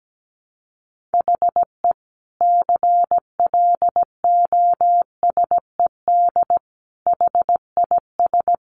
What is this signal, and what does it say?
Morse code, heard as a keyed tone, about 17 wpm, 700 Hz. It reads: HE CLOSED HIS
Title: Sentences from Top 1400 Words 17wpm